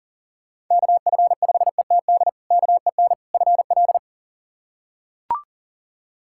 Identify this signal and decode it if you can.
Morse code, heard as a keyed tone, about 40 wpm, 700 Hz. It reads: KF5ETD KEN FL
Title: K1USN SST QSO Practice - 40wpm